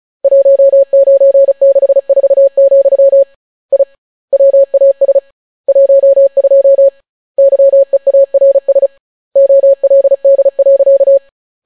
Morse code, heard as a keyed tone, about 36 wpm, 550 Hz. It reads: 1964, I WAS 12 YEARS OLD.